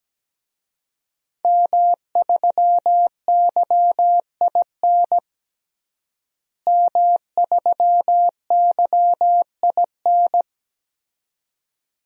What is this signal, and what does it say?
Morse code, heard as a keyed tone, about 17 wpm, 700 Hz. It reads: M3YIN M3YIN
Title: DXCC-100 Call Signs - 5 Characters - Repeated 2x - 17wpm